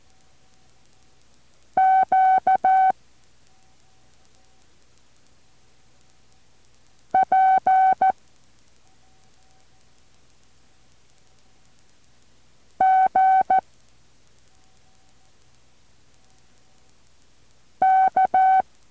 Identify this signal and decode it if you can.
Morse code, heard as a keyed tone, about 14 wpm, 760 Hz. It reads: Q P G K
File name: intl morse code2